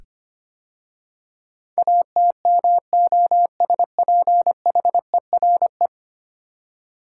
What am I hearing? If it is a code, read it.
Morse code, heard as a keyed tone, about 25 wpm, 700 Hz. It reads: ATMOSPHERE
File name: 10 Letter Words - First 3 Letters Given - 25wpm